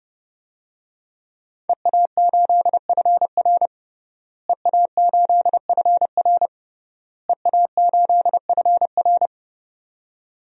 Morse code, heard as a keyed tone, about 31 wpm, 700 Hz. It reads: EA8FR EA8FR EA8FR
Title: DXCC-100 Call Signs - 5 Characters - Repeated 3x - 30wpm